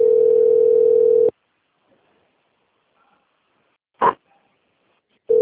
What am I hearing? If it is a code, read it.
Ringback tone (North America), 440 + 480 Hz.
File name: out- - - - - .49772